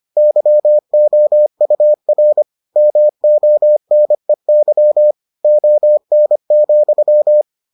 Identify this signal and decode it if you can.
Morse code, heard as a keyed tone, about 25 wpm, 600 Hz. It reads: YOUR MONEY ON,